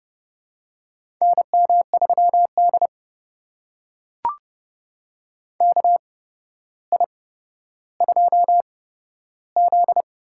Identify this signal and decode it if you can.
Morse code, heard as a keyed tone, about 30 wpm, spaced as for 13 wpm, 700 Hz. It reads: NM3D KI2Z
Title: USA Call Signs - 4 Characters - 8x Character Spacing - 30wpm